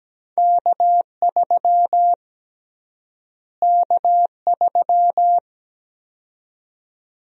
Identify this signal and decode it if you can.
Morse code, heard as a keyed tone, about 17 wpm, 700 Hz. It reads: K3 K3
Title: USA Call Sign Prefixes - Repeated 2x - 17wpm